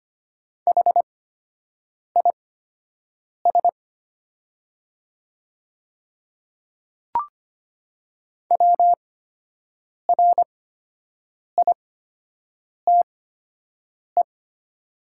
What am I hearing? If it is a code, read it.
Morse code, heard as a keyed tone, about 25 wpm, spaced as for 7 wpm, 700 Hz. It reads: HIS WRITE
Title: 8x Character Spacing - Top 300 Words - 25wpm